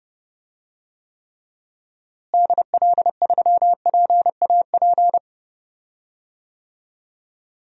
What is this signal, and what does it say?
Morse code, heard as a keyed tone, about 30 wpm, 700 Hz. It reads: DL3PAP
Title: DXCC-100 Call Signs - 6 Characters - Repeated 2x - 30wpm